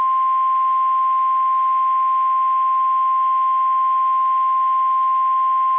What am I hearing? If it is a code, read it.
Emergency Alert System (SAME) tones (the 1050 Hz attention tone). It is heard from 0.4 to 5.1 s.
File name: eas / KSDZKDJL / KSDZKDJL